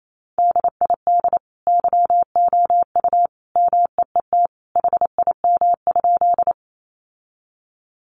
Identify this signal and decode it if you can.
Morse code, heard as a keyed tone, about 28 wpm, 700 Hz. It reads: DID YOU MEET HIM?
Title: Sentences from Top 1100 Words 28wpm